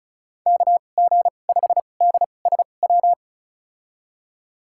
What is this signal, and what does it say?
Morse code, heard as a keyed tone, about 34 wpm, spaced as for 25 wpm, 690 Hz. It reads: KG5DSW
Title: USA Call Signs - 6 Characters - 2x Character Spacing - 35wpm